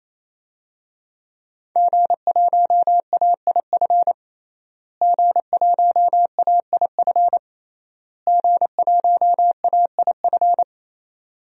Morse code, heard as a keyed tone, about 28 wpm, 700 Hz. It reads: G1AIF G1AIF G1AIF
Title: DXCC-100 Call Signs - 5 Characters - Repeated 3x - 28wpm